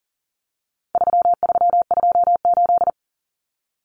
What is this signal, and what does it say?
Morse code, heard as a keyed tone, about 40 wpm, 700 Hz. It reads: 3328
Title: Common Cut Numbers - 4-Digit 40wpm